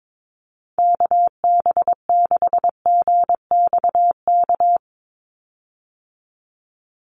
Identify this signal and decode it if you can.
Morse code, heard as a keyed tone, about 22 wpm, 700 Hz. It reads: KB6GXK
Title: USA Call Signs 22wpm